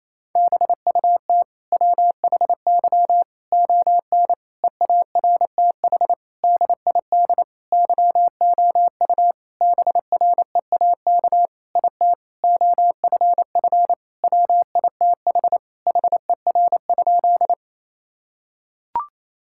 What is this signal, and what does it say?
Morse code, heard as a keyed tone, about 27 wpm, 700 Hz. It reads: BUT WHY ON EARTH DID YOU BREAK IT OFF WITH HER?
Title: Sentences from Top 1000 Words - Review Part 6 - 28wpm